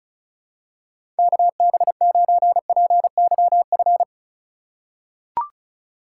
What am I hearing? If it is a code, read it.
Morse code, heard as a keyed tone, about 34 wpm, 700 Hz. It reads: KB9PYF
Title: USA Call Signs - 6 Characters - 3x Character Spacing - 35wpm